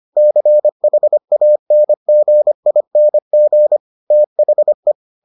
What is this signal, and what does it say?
Morse code, heard as a keyed tone, about 25 wpm, 600 Hz. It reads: CHANGING THE